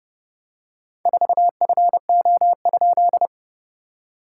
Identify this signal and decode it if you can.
Morse code, heard as a keyed tone, about 30 wpm, 700 Hz. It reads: 4FO?
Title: SOTA QSO Practice 30wpm